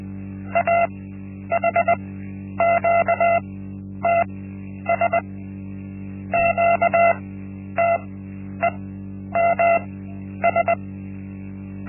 Morse code, heard as a keyed tone, about 20 wpm, spaced as for 10 wpm, 680 Hz. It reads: AHQTS QTEMS
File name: Linksmakalnis-1